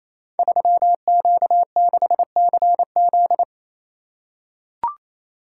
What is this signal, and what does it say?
Morse code, heard as a keyed tone, about 28 wpm, 700 Hz. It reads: 3Q6CZ
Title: Code Groups - 5 Letter:Numbers - 28wpm